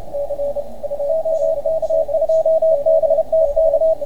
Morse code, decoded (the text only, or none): OK2BTK